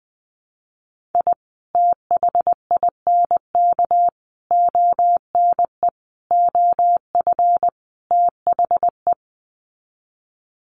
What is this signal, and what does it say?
Morse code, heard as a keyed tone, about 20 wpm, 700 Hz. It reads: I THINK ONE OF THE
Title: Sets of 5 Words 20wpm